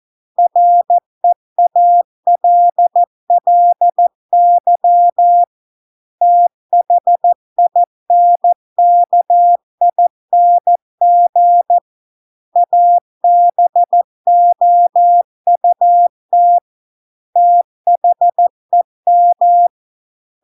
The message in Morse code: REALLY THINKING ABOUT THEM

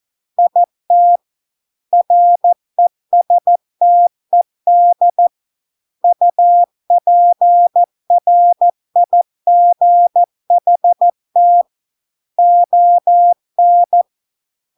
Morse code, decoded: IT RESTED UPRIGHT ON

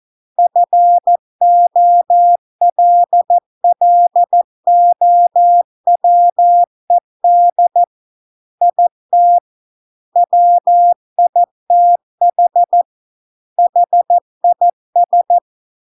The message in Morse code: FOLLOWED IT WITH HIS

